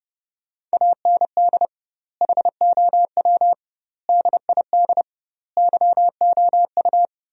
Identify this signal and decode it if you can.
Morse code, heard as a keyed tone, about 32 wpm, 700 Hz. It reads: AND HOW DID YOU
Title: 2x Word-Spacing - Sentences from Top 100 Words - 30wpm